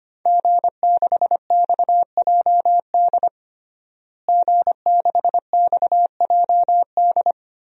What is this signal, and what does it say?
Morse code, heard as a keyed tone, about 25 wpm, 700 Hz. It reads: G6XJD G6XJD
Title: DXCC-100 Call Signs - 5 Characters - Repeated 2x - 25wpm